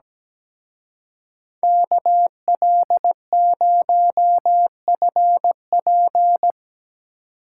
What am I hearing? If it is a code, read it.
Morse code, heard as a keyed tone, about 18 wpm, 700 Hz. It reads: KL0FP
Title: Speed Racing - USA 2x2 Call Signs 17wpm